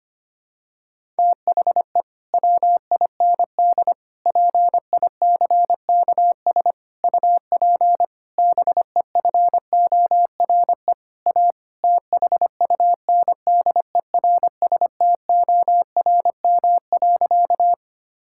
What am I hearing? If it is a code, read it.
Morse code, heard as a keyed tone, about 25 wpm, 700 Hz. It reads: THE WIND PICKS UP BEFORE A THUNDERSTORM.
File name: practice / 25wpm / Difficult Sentences with Hints - 25wpm